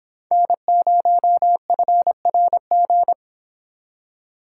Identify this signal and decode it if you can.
Morse code, heard as a keyed tone, about 26 wpm, 700 Hz. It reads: N0FRG